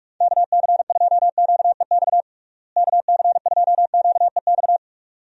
Morse code, heard as a keyed tone, about 46 wpm, 700 Hz. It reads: KC2QEX KC2QEX
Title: USA Call Signs - 6 Characters - Repeated 2x - 45wpm